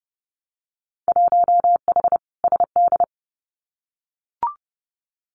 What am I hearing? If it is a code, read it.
Morse code, heard as a keyed tone, about 30 wpm, 700 Hz. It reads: 1H SD
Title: Winter Field Day Exchanges 30wpm